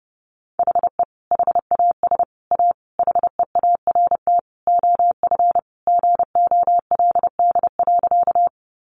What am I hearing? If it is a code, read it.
Morse code, heard as a keyed tone, about 30 wpm, 700 Hz. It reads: HE HAS A HEART OF GOLD.
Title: Sentences from Top 1200 Words 30wpm